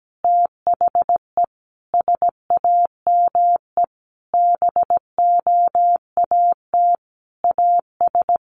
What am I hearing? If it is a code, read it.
Morse code, heard as a keyed tone, about 17 wpm, 700 Hz. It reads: THE SAME BOAT AS
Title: Sentences from Top 500 Words 17wpm